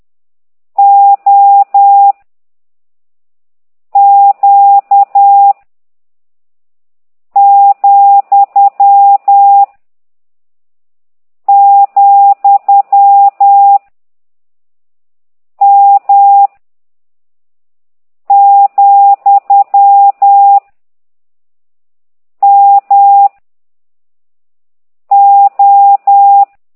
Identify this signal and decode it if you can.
Morse code, heard as a keyed tone, about 10 wpm, 800 Hz. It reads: O Q , , M , M O
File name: ct2idw / exerc15